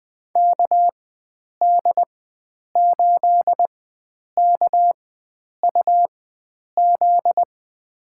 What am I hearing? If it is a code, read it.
Morse code, heard as a keyed tone, about 20 wpm, spaced as for 9 wpm, 700 Hz. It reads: KD8KUZ
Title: USA Call Signs - 6 Characters - 4x Character Spacing - 20wpm